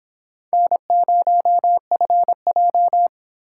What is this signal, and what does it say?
Morse code, heard as a keyed tone, about 26 wpm, 700 Hz. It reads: N0FJ